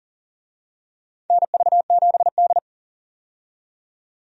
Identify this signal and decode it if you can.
Morse code, heard as a keyed tone, about 40 wpm, 700 Hz. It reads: NV7D
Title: USA Call Signs - 4 Characters - 40wpm